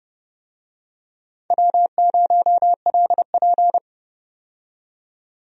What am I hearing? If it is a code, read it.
Morse code, heard as a keyed tone, about 30 wpm, 700 Hz. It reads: W0LP
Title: USA Call Signs - 4 Characters - 30wpm